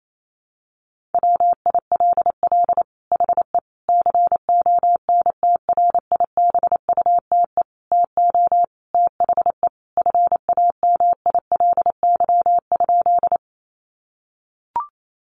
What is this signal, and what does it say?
Morse code, heard as a keyed tone, about 28 wpm, 700 Hz. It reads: WILL HE CONTRIBUTE TO THE FAMILY?
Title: Sentences from Top 1400 Words 28wpm